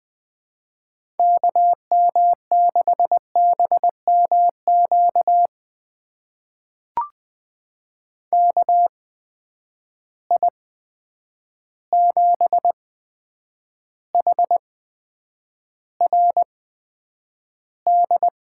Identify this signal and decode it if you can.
Morse code, heard as a keyed tone, about 20 wpm, spaced as for 9 wpm, 700 Hz. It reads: KM6BMQ KI7HRD
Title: USA Call Signs - 6 Characters - 8x Character Spacing - 20wpm